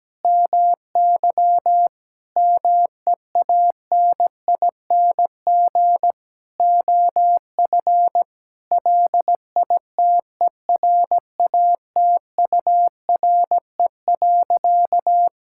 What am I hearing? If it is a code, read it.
Morse code, heard as a keyed tone, about 17 wpm, 700 Hz. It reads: MY MEANING OF LITERATURE.
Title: Sentences from Top 1000 Words - Review Part 6 - 17wpm